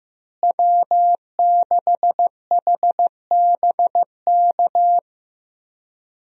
Morse code, heard as a keyed tone, about 15 wpm, 700 Hz. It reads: W6HBK
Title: USA Call Signs - 5 Characters - Repeated 3x - 15wpm